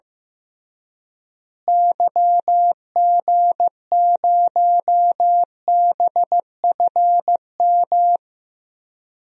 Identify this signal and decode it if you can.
Morse code, heard as a keyed tone, about 15 wpm, 700 Hz. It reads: YG0BFM
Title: DXCC-100 Call Signs - 6 Characters - Repeated 2x - 15wpm